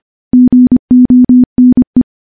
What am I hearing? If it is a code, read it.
Morse code, heard as a keyed tone, about 25 wpm, 250 Hz. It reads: GONE
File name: morse / morse-training-25WPM